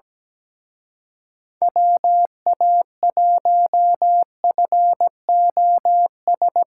Morse code, heard as a keyed tone, about 20 wpm, 700 Hz. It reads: WA1FOS
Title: Speed Racing - USA 2x3 Call Signs 17wpm